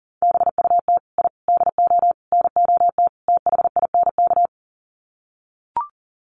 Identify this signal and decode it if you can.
Morse code, heard as a keyed tone, about 40 wpm, 700 Hz. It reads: BUT I DO NOT THINK
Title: Sets of 5 Words 40wpm